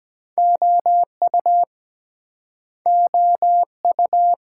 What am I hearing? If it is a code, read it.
Morse code, heard as a keyed tone, about 22 wpm, 700 Hz. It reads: OU OU